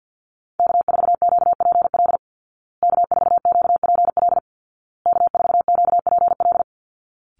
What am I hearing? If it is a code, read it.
Morse code, heard as a keyed tone, about 48 wpm, 700 Hz. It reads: K4QPL K4QPL K4QPL
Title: Call Signs Repeated 2x - 50wpm